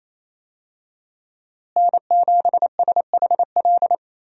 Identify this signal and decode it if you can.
Morse code, heard as a keyed tone, about 28 wpm, 700 Hz. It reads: N7SHL
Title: USA Call Signs - 5 Characters - Repeated 2x - 28wpm